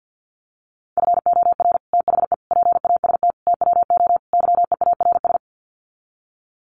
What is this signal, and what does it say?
Morse code, heard as a keyed tone, about 48 wpm, 700 Hz. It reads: FOR THE PAST TWO YEARS
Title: Sets of 5 Words Encore2 50wpm